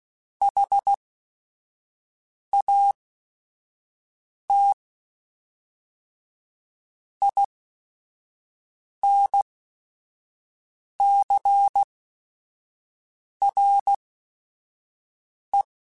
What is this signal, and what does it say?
Morse code, heard as a keyed tone, about 16 wpm, spaced as for 5 wpm, 800 Hz. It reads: HAT INCRE
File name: Technical Subjects / Technical Subjects-mp3s / XC @5